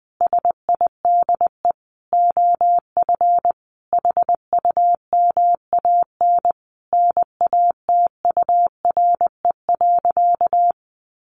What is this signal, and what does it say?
Morse code, heard as a keyed tone, about 20 wpm, 700 Hz. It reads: SIDE OF HUMAN NATURE.
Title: Sentences from Top 700 Words 20wpm